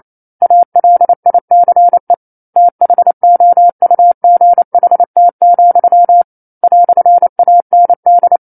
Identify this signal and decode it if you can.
Morse code, heard as a keyed tone, about 28 wpm, 700 Hz. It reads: ALICE THOUGHT, "AND
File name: 09-Chapter-09 28 WPM